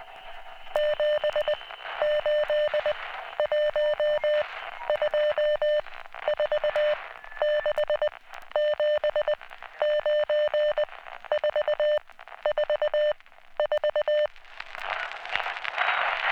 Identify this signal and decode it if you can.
Morse code, heard as a keyed tone, about 20 wpm, 600 Hz. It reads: 7 8 1 2 4 6 7 9 4 4 4